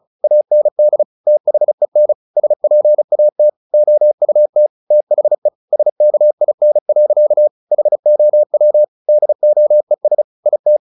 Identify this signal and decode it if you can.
Morse code, heard as a keyed tone, about 34 wpm, 600 Hz. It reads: AND THEN SPAT OUT THE SKIN. HOW DOES IT
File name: the old man and the sea / the old man and the sea 1 35wpm